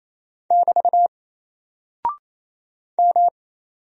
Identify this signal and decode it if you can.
Morse code, heard as a keyed tone, about 27 wpm, 700 Hz. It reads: <BT> M
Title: Single Character (Letter:Number:Punctuation:Prosign) Alt 28wpm